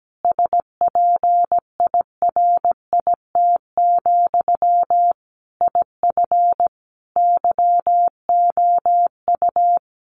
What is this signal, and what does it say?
Morse code, heard as a keyed tone, about 17 wpm, 700 Hz. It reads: SPIRIT, IF YOU
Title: Sentences from Top 1400 Words 17wpm